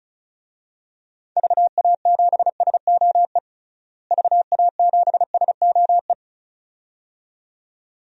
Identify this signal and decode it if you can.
Morse code, heard as a keyed tone, about 36 wpm, 700 Hz. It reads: VA7SOE VA7SOE
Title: DXCC-100 Call Signs - 6 Characters - Repeated 2x - 35wpm